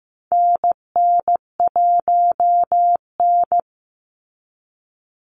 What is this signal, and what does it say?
Morse code, heard as a keyed tone, about 15 wpm, 700 Hz. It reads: NN1N
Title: Call Signs Repeated 3x - 15wpm